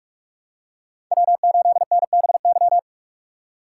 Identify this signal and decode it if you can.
Morse code, heard as a keyed tone, about 41 wpm, 700 Hz. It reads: W8NBY